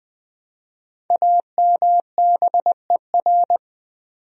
Morse code, heard as a keyed tone, about 20 wpm, 700 Hz. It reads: AMBER
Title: USA Common Names - Max 5 Letters - 20wpm